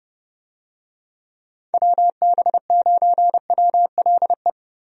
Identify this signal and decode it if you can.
Morse code, heard as a keyed tone, about 31 wpm, 700 Hz. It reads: WB9WLE
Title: USA Call Signs - 6 Characters - Repeated 3x - 30wpm